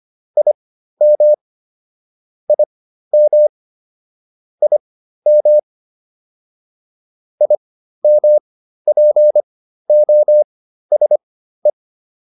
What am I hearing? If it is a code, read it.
Morse code, heard as a keyed tone, about 25 wpm, spaced as for 13 wpm, 600 Hz. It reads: IM IM IM IMPOSE